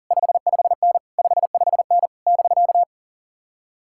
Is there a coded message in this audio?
Yes: Morse code from 0.1 to 2.8 s, heard as a keyed tone, about 52 wpm, 700 Hz. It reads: 55N 55N <BK>